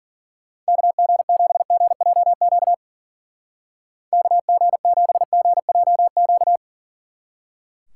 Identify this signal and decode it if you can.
Morse code, heard as a keyed tone, about 49 wpm, 700 Hz. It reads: KG7GJQ KG7GJQ
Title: Speed Racing - USA 2x3 Call Signs 40wpm